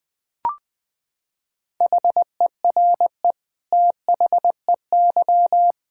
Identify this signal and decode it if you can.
Morse code, heard as a keyed tone, about 19 wpm, 700 Hz. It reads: HERE THEY
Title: Sentences from Top 1600 Words 20wpm